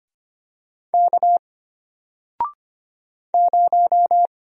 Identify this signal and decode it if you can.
Morse code, heard as a keyed tone, about 25 wpm, 700 Hz. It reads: K 0
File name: Single-Letter-Number-25wpm